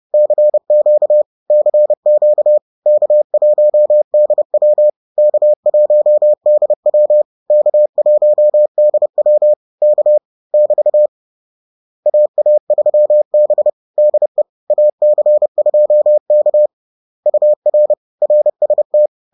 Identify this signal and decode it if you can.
Morse code, heard as a keyed tone, about 30 wpm, 600 Hz. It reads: CQ CQ K1DW K1DW K1DW K <BT> AA3B DE AC2K UR RST
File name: 30wpm